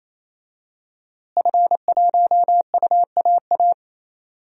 Morse code, heard as a keyed tone, about 28 wpm, 700 Hz. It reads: F1UAA
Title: DXCC-100 Call Signs - 5 Characters - Repeated 3x - 28wpm